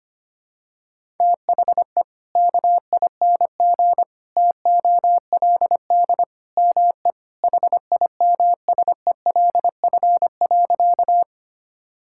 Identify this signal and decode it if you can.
Morse code, heard as a keyed tone, about 25 wpm, 700 Hz. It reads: THE KING TOLD ME HIMSELF.
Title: Sentences from Top 1600 Words 25wpm